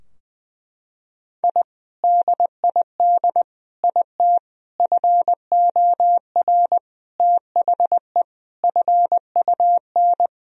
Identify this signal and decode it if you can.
Morse code, heard as a keyed tone, about 21 wpm, 700 Hz. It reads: I DID IT FOR THE FUN
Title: Sentences from Top 1000 Words - Review Part 4 - 20wpm